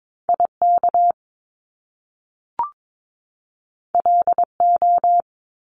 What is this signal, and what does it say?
Morse code, heard as a keyed tone, about 22 wpm, 700 Hz. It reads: IK LO
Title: Top Two Letter Combinations in Words 22wpm